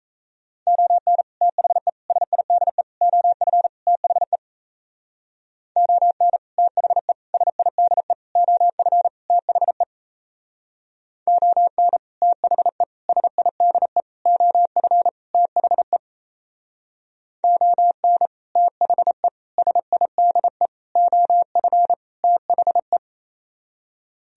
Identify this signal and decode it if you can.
Morse code, heard as a keyed tone, about 33 wpm, 700 Hz. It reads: ON THE SIDE OF THE ON THE SIDE OF THE ON THE SIDE OF THE ON THE SIDE OF THE
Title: Speed-Racing - Sets of 5 Words 28wpm